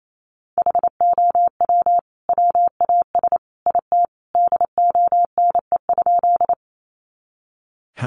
Morse code, heard as a keyed tone, about 28 wpm, 700 Hz. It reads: HOW WAS IT DONE?
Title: Sentences from Top 500 Words 28wpm